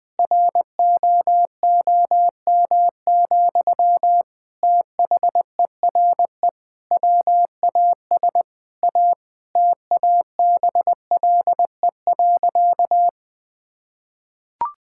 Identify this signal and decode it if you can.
Morse code, heard as a keyed tone, about 20 wpm, 700 Hz. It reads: ROOM, THERE WAS A TABLE.
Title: Sentences from Top 1000 Words - Review Part 6 - 20wpm